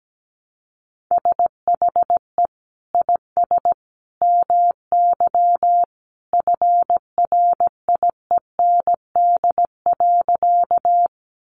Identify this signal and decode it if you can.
Morse code, heard as a keyed tone, about 17 wpm, 700 Hz. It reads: SHE IS MY FRIEND.
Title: Sentences from Top 300 Words 17wpm